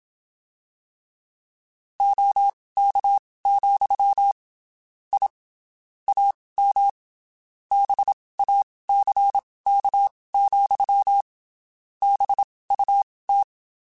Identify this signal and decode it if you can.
Morse code, heard as a keyed tone, about 27 wpm, spaced as for 19 wpm, 800 Hz. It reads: OK, I AM BACK, BUT
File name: (spaced) / QSOs Spaced-mp3s / kc7sac @25 sp